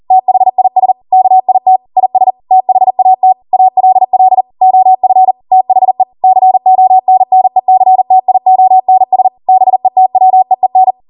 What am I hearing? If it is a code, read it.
Morse code, heard as a keyed tone, about 40 wpm, 750 Hz. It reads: THIS KIT IS THAT ALL OF THE CONNECTIONS BETWEEN
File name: Morse / 40WPM